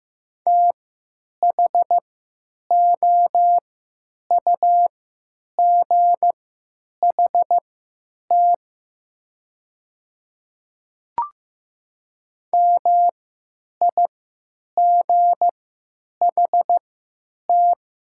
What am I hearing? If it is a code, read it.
Morse code, heard as a keyed tone, about 15 wpm, spaced as for 9 wpm, 700 Hz. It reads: THOUGHT MIGHT
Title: 3x Character Spacing - Top 300 Words - 15wpm